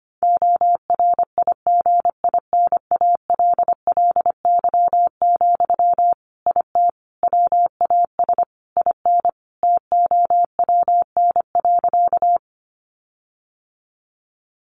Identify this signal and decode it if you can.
Morse code, heard as a keyed tone, about 25 wpm, 700 Hz. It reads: ORIGINALLY, IT WAS IN TOWN.
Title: Sentences from Top 1500 Words 25wpm